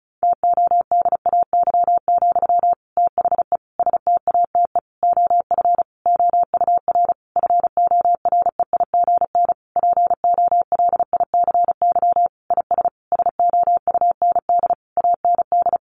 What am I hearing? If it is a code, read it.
Morse code, heard as a keyed tone, about 34 wpm, 700 Hz. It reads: TODAY, THE STATE OF OUR FOREIGN POLICY IS SOUND AND
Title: Sentences from Top 1400 Words 35wpm